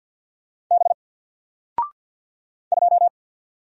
Morse code, heard as a keyed tone, about 50 wpm, 700 Hz. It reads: B 2
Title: Single Letter-Number 50wpm